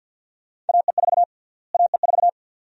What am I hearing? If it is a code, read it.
Morse code, heard as a keyed tone, about 47 wpm, 700 Hz. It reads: AE4 AE4
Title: USA Call Sign Prefixes - Repeated 2x - 50wpm